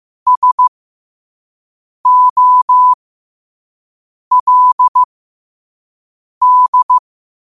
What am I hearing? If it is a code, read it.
Morse code, heard as a keyed tone, about 15 wpm, spaced as for 5 wpm, 1000 Hz. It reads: SOLD
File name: qotdfiles / 5wpm